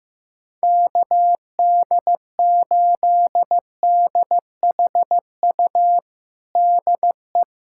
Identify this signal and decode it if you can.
Morse code, heard as a keyed tone, about 16 wpm, 700 Hz. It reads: KD8DHU DE
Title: USA Callsign DE Callsign 15wpm